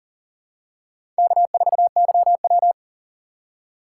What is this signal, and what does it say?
Morse code, heard as a keyed tone, about 37 wpm, 700 Hz. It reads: K4YW
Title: USA Call Signs - 4 Characters - 4x Character Spacing - 40wpm